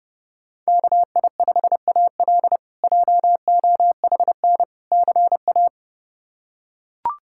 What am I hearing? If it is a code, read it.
Morse code, heard as a keyed tone, about 30 wpm, 700 Hz. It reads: KI5AL JOHN CA
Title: K1USN SST QSO Practice - 30wpm